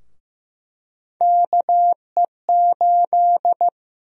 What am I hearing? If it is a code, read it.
Morse code, heard as a keyed tone, about 16 wpm, 700 Hz. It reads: KE8
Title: USA Call Sign Prefixes - Repeated 2x - 15wpm